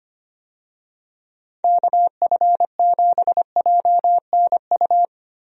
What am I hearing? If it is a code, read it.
Morse code, heard as a keyed tone, about 25 wpm, 700 Hz. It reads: KF7JNU
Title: USA Call Signs - 6 Characters - Repeated 2x - 25wpm